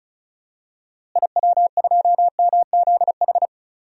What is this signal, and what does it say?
Morse code, heard as a keyed tone, about 36 wpm, 700 Hz. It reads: IW2MZH
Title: DXCC-100 Call Signs - 6 Characters - Repeated 3x - 35wpm